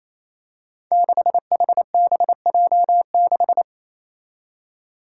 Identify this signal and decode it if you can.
Morse code, heard as a keyed tone, about 28 wpm, 700 Hz. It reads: 6HBJ6
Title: Five Challenging Letter-Number Combinations 28wpm